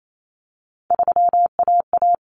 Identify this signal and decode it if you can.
Morse code, heard as a keyed tone, about 28 wpm, 700 Hz. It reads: 3AA
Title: Common Cut Numbers - 3-Digit 28wpm